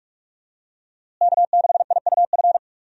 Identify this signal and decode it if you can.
Morse code, heard as a keyed tone, about 42 wpm, 700 Hz. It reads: K6IUF
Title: USA Call Signs - 5 Characters - Repeated 2x - 45wpm